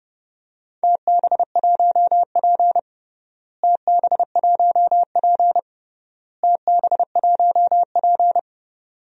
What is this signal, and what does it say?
Morse code, heard as a keyed tone, about 30 wpm, 700 Hz. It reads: TB1P TB1P TB1P
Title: DXCC-100 Call Signs - 4 Characters - Repeated 3x - 30wpm